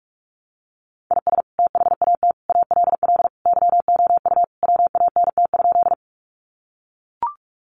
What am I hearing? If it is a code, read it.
Morse code, heard as a keyed tone, about 44 wpm, 700 Hz. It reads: IS THAT ALL YOU WANT?
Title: Sentences from Top 200 Words 45wpm